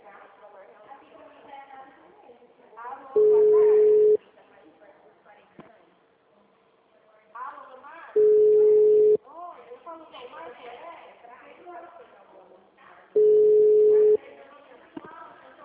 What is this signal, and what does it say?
Ringback tone (Europe, 425 Hz), with speech.